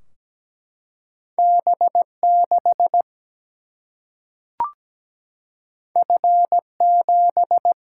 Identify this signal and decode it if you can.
Morse code, heard as a keyed tone, about 17 wpm, 700 Hz. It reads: B6 F7
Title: Single-Letter Single-Number 17wpm